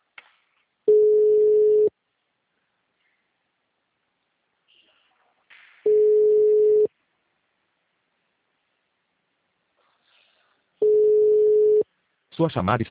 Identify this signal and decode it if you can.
Ringback tone (Europe, 425 Hz), with speech.
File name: out- -2020- - - .47111